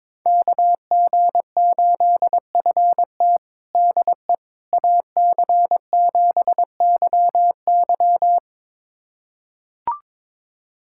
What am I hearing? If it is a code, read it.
Morse code, heard as a keyed tone, about 22 wpm, 700 Hz. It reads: KG8FT DE AC7YY
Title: USA Callsign DE Callsign - 5 Characters - 22wpm